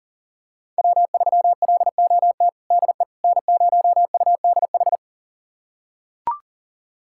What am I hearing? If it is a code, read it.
Morse code, heard as a keyed tone, about 40 wpm, 700 Hz. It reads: W3LOT DE N0UDH
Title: USA Callsign DE Callsign - 5 Characters - 40wpm